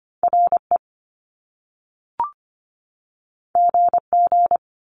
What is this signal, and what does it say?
Morse code, heard as a keyed tone, about 25 wpm, 700 Hz. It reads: RE GG
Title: Top Two Letter Combinations in Words 25wpm